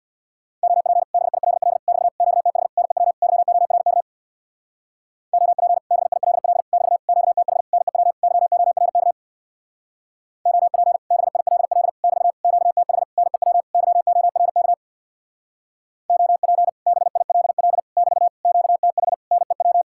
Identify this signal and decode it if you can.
Morse code, heard as a keyed tone, about 61 wpm, 700 Hz. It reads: OP BILL <BT> QTH NEW YORK OP BILL <BT> QTH NEW YORK OP BILL <BT> QTH NEW YORK OP BILL <BT> QTH NEW
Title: Speed-Racing Common QSO Phrases - 50wpm